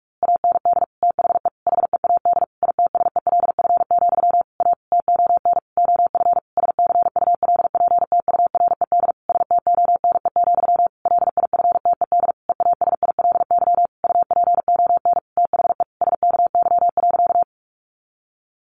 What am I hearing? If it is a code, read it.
Morse code, heard as a keyed tone, about 44 wpm, 700 Hz. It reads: AND THE HEAD ITSELF, A TON OF SCULPTURED STONE, LIFTED EASILY UPON THE SKY.
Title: A Sound Of Thunder 45wpm